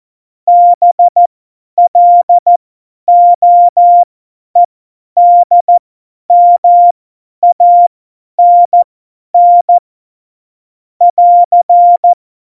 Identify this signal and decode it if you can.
Morse code, heard as a keyed tone, about 14 wpm, spaced as for 10 wpm, 700 Hz. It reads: BLOEDMANN <AR>
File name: Morsekurs / Witze / Witz 06. Februar 2018, 50 F70